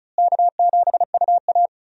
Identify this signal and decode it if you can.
Morse code, heard as a keyed tone, about 36 wpm, 700 Hz. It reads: K7UA
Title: USA Call Signs - 4 Characters - Repeated 3x - 35wpm